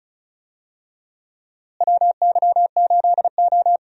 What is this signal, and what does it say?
Morse code, heard as a keyed tone, about 36 wpm, 700 Hz. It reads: WY8O
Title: USA Call Signs - 4 Characters - Repeated 3x - 35wpm